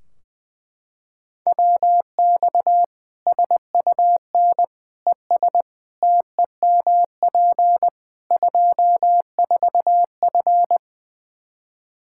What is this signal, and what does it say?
Morse code, heard as a keyed tone, about 20 wpm, 700 Hz. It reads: WX SUN ES TEMP 24F
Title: Practice QSO - Exchange 2 - Info - 20wpm